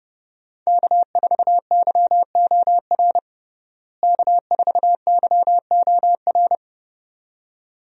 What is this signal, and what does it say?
Morse code, heard as a keyed tone, about 30 wpm, 700 Hz. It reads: K4YOR K4YOR
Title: USA Call Signs - 5 Characters - Repeated 2x - 30wpm